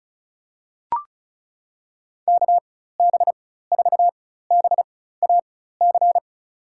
Morse code, heard as a keyed tone, about 34 wpm, spaced as for 16 wpm, 700 Hz. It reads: KB4BAC
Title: USA Call Signs - 6 Characters - 4x Character Spacing - 35wpm